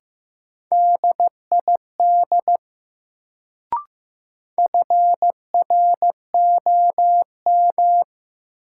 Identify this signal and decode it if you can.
Morse code, heard as a keyed tone, about 15 wpm, 700 Hz. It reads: DID FROM
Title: Sentences from Top 100 Words - New Word Drill - 15wpm